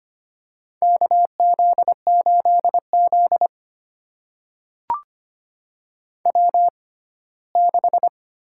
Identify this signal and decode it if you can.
Morse code, heard as a keyed tone, about 25 wpm, spaced as for 17 wpm, 700 Hz. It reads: KZ8Z W6
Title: USA Call Signs - 4 Characters - 6x Character Spacing - 25wpm